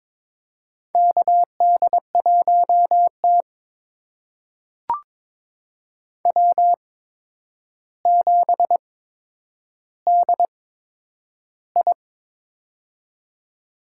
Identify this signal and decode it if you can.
Morse code, heard as a keyed tone, about 22 wpm, spaced as for 9 wpm, 700 Hz. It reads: KD1T W7DI
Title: USA Call Signs - 4 Characters - 8x Character Spacing - 22wpm